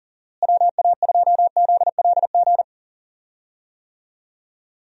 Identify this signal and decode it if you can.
Morse code, heard as a keyed tone, about 40 wpm, 700 Hz. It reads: WA2ZLG
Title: USA Call Signs - 6 Characters - Repeated 3x - 40wpm